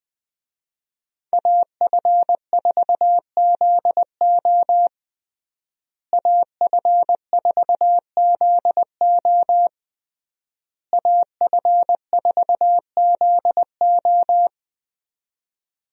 Morse code, heard as a keyed tone, about 21 wpm, 700 Hz. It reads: AF4ZO AF4ZO AF4ZO
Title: USA Call Signs - 5 Characters - Repeated 3x - 20wpm